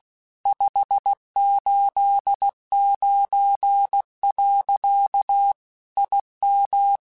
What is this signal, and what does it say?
Morse code, heard as a keyed tone, about 16 wpm, 800 Hz. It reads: 589. IM
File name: QSOs / QSOs-mp3s / WD0FBT 1@15